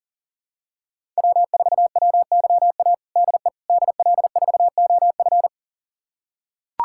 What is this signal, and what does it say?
Morse code, heard as a keyed tone, about 40 wpm, 700 Hz. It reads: W4WYA DE DL4OF
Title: USA Callsign DE Callsign - 5 Characters - 40wpm